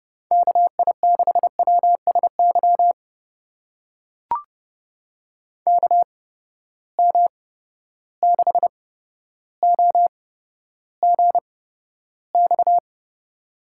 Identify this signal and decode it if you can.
Morse code, heard as a keyed tone, about 30 wpm, spaced as for 13 wpm, 700 Hz. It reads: KI6WSY KM6OGX
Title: USA Call Signs - 6 Characters - 8x Character Spacing - 30wpm